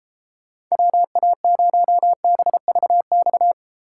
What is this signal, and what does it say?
Morse code, heard as a keyed tone, about 38 wpm, 700 Hz. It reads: WA0BVX